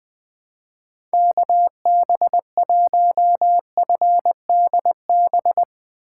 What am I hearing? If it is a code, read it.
Morse code, heard as a keyed tone, about 20 wpm, 700 Hz. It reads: KB1FDB
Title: USA Call Signs - 6 Characters - 3x Character Spacing - 20wpm